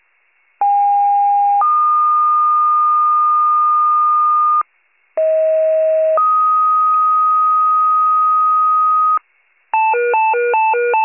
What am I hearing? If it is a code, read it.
Two-tone page, heard as a one-second tone, then a three-second tone, about 810 then 1220 Hz, about 640 then 1250 Hz. It is heard from 0.6 to 9.2 s.